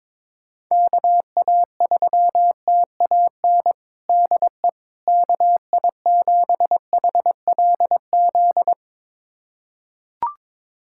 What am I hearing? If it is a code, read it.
Morse code, heard as a keyed tone, about 22 wpm, 700 Hz. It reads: KA3TAN DE KI7HLZ
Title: USA Callsign DE Callsign - 6 Characters - 22wpm